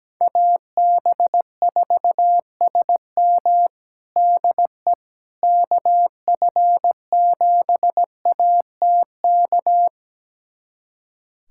Morse code, heard as a keyed tone, about 17 wpm, 700 Hz. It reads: AB4SM DE KF7ATK
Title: USA Callsign DE Callsign 17wpm